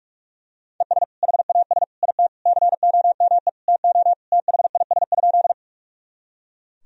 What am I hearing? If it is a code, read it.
Morse code, heard as a keyed tone, about 41 wpm, 700 Hz. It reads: ES HAS IT COME TO THIS?
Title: Sentences from Top 100 Words and Abbreviations 45wpm